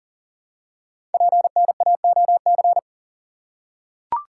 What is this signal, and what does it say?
Morse code, heard as a keyed tone, about 40 wpm, 700 Hz. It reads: PNAOC
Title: Code Groups - 5 Letters - 40wpm